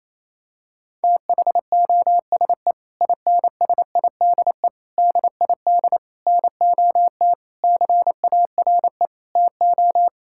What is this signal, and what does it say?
Morse code, heard as a keyed tone, about 28 wpm, 700 Hz. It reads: THOSE INSIDE DID NOT CARE TO
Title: Sentences from Top 1000 Words - Review Part 6 - 28wpm